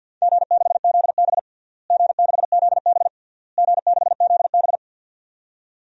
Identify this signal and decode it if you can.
Morse code, heard as a keyed tone, about 46 wpm, 700 Hz. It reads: G6ZB G6ZB G6ZB
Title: DXCC-100 Call Signs - 4 Characters - Repeated 3x - 50wpm